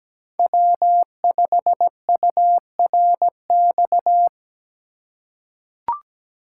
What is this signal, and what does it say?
Morse code, heard as a keyed tone, about 17 wpm, 700 Hz. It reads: W5URX
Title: USA Call Signs - 5 Characters - 4x Character Spacing - 17wpm